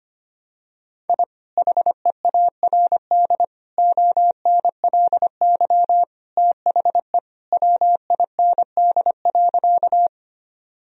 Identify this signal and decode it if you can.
Morse code, heard as a keyed tone, about 25 wpm, 700 Hz. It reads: I HEARD ONLY THE WIND.
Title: Sentences from Top 1000 Words - Review Part 2 - 25wpm